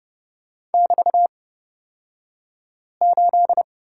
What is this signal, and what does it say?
Morse code, heard as a keyed tone, about 30 wpm, spaced as for 5 wpm, 700 Hz. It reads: <BT>8
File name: Lesson 37d - Letter x - Two Characters 30wpm